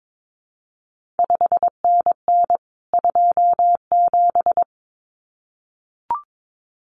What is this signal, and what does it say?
Morse code, heard as a keyed tone, about 22 wpm, 700 Hz. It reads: 5NN 27
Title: CQ WW DX Contest Exchange - 22wpm